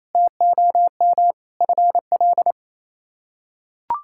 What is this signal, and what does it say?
Morse code, heard as a keyed tone, about 28 wpm, 700 Hz. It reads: TOM FL
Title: K1USN SST QSO Practice - 28wpm